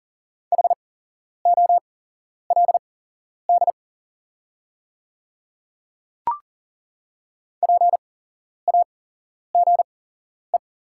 Morse code, heard as a keyed tone, about 39 wpm, spaced as for 11 wpm, 690 Hz. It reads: HOLD PAGE
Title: 8x Character Spacing - Top 500 Words - 40wpm